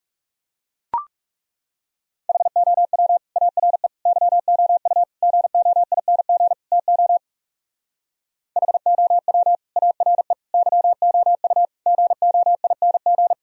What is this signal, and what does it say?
Morse code, heard as a keyed tone, about 37 wpm, 700 Hz. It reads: HOW ARE YOU GOING TO HOW ARE YOU GOING
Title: Speed-Racing - Sets of 5 Words 30wpm